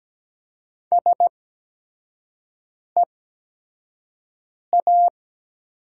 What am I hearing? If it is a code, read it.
Morse code, heard as a keyed tone, about 17 wpm, spaced as for 5 wpm, 700 Hz. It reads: SEA
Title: 8x Character Spacing - Top 300 Words - 17wpm